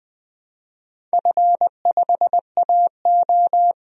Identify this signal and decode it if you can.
Morse code, heard as a keyed tone, about 20 wpm, 700 Hz. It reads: F5AO
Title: DXCC-100 Call Signs - 4 Characters - 20wpm